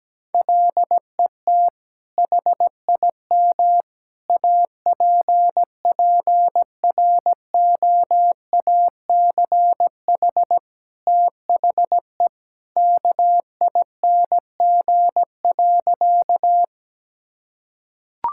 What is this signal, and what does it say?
Morse code, heard as a keyed tone, about 17 wpm, 700 Hz. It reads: LET HIM APPROACH THE KING.
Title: Sentences from Top 1000 Words - Review Part 6 - 17wpm